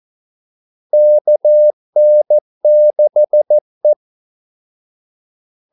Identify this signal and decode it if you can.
Morse code, heard as a keyed tone, about 14 wpm, 600 Hz. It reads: KN6E
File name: Callsigns-14wpm